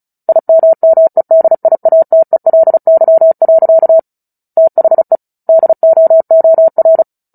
Morse code, heard as a keyed tone, about 34 wpm, 650 Hz. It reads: IMMEDIATELY. THE DOOR
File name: Book Two / 09-Chapter 9 - 35 WPM